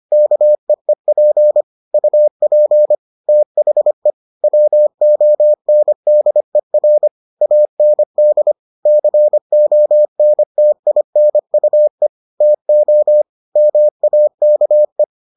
Morse code, heard as a keyed tone, about 25 wpm, 600 Hz. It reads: KEEP UP THE WONDER AND CONTINUE TO MAKE